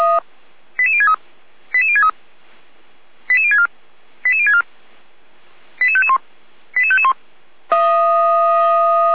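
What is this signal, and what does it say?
Five-tone selective call (ZVEI), heard as quick runs of five tones, tones of about 70 ms. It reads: 88063 88063 88064 88064 88551 88551
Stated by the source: ZVEI five-tone selective call, received on SDR.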